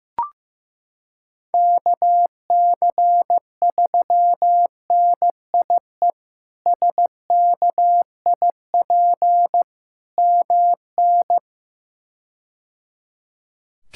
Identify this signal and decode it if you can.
Morse code, heard as a keyed tone, about 15 wpm, 700 Hz. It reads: KC3NIE SKIP MN
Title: K1USN SST QSO Practice - 15wpm